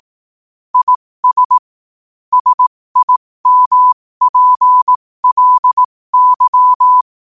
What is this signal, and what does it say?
Morse code, heard as a keyed tone, about 18 wpm, spaced as for 15 wpm, 1000 Hz. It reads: IS SIMPLY
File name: qotdfiles / 15wpm